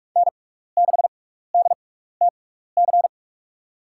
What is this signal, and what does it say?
Morse code, heard as a keyed tone, about 46 wpm, spaced as for 15 wpm, 700 Hz. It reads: N6DTC
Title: USA Call Signs - 6x Character Spacing - 45wpm